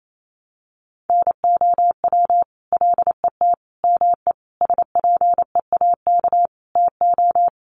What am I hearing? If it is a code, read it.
Morse code, heard as a keyed tone, about 28 wpm, 700 Hz. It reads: NOW LET ME SPEAK TO
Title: Sentences from Top 1200 Words 28wpm